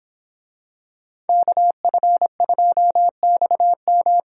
Morse code, heard as a keyed tone, about 26 wpm, 700 Hz. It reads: KF2XM
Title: Speed Racing - USA 2x2 Call Signs 22wpm